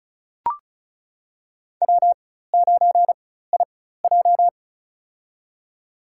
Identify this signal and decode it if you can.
Morse code, heard as a keyed tone, about 34 wpm, spaced as for 16 wpm, 700 Hz. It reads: W9IJ
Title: USA Call Signs - 4 Characters - 4x Character Spacing - 35wpm